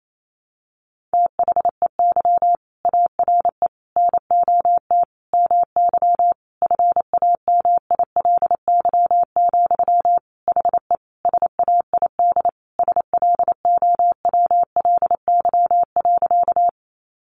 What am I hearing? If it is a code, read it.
Morse code, heard as a keyed tone, about 28 wpm, 700 Hz. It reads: THEY ARE NOT MY FAMILY, HE SAID SLOWLY.
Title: Sentences from Top 1100 Words 28wpm